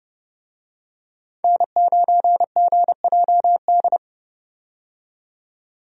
Morse code, heard as a keyed tone, about 30 wpm, 700 Hz. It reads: N9GJD
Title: USA Call Signs - 5 Characters - 30wpm